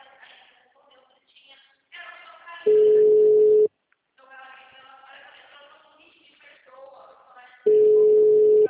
Ringback tone (Europe, 425 Hz).